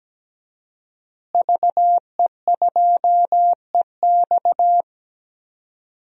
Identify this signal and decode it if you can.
Morse code, heard as a keyed tone, about 17 wpm, 700 Hz. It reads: VE2EX
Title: DXCC-100 Call Signs - 5 Characters - Repeated 2x - 17wpm